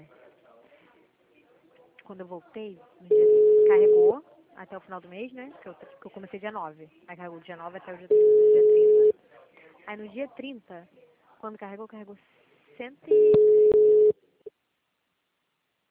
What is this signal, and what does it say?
Ringback tone (Europe, 425 Hz), with speech.